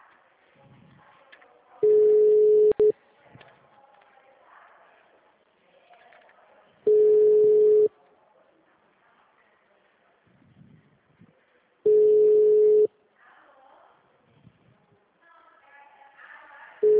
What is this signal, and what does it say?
Ringback tone (Europe, 425 Hz), with speech.
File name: out- -2003- - - .37247